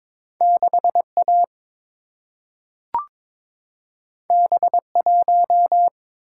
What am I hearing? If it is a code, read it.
Morse code, heard as a keyed tone, about 22 wpm, 700 Hz. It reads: 6A B1
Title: Single-Letter Single-Number 22wpm